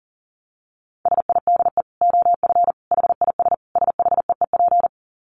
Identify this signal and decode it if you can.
Morse code, heard as a keyed tone, about 40 wpm, 700 Hz. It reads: SIDE OF HIS SHEEP
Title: Dit Challenge 40wpm